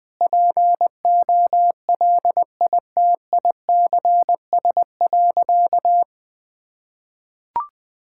Morse code, heard as a keyed tone, about 20 wpm, 700 Hz. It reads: POLITICS.
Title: Sentences from Top 1000 Words - Review Part 6 - 20wpm